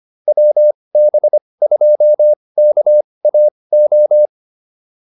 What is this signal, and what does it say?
Morse code, heard as a keyed tone, about 25 wpm, spaced as for 20 wpm, 590 Hz. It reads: WB2KAO
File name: POTA208 20